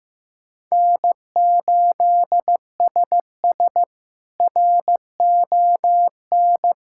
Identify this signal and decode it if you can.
Morse code, heard as a keyed tone, about 15 wpm, 700 Hz. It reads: N8SS RON
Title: K1USN SST QSO Practice - 15wpm